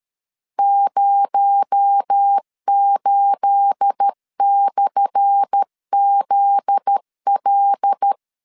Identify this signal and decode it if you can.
Morse code, heard as a keyed tone, about 13 wpm, 800 Hz. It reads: 08/ZL
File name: LEZIONI / 10 SIGNS / 10A SLASH 12wpm 2314